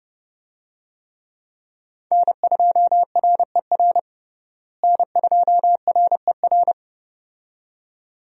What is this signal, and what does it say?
Morse code, heard as a keyed tone, about 32 wpm, 700 Hz. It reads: N2RER N2RER
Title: USA Call Signs - 5 Characters - Repeated 2x - 30wpm